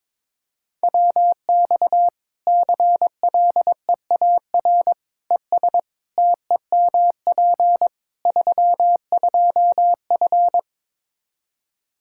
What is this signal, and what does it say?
Morse code, heard as a keyed tone, about 23 wpm, 700 Hz. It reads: WX CLEAR ES TEMP 32F
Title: Practice QSO - Exchange 2 - Info - 22wpm